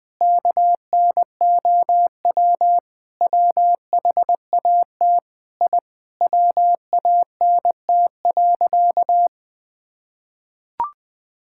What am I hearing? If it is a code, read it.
Morse code, heard as a keyed tone, about 21 wpm, 700 Hz. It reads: KNOW WHAT I WANT.
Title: Sentences with 5-Word Phrases - 20wpm